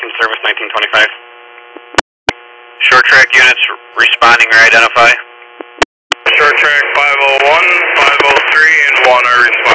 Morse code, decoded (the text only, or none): none